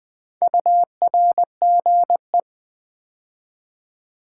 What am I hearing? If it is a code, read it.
Morse code, heard as a keyed tone, about 20 wpm, 700 Hz. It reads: URGE
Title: Build A Word - 2 Letters At A Time - 4-Letter Words - 20wpm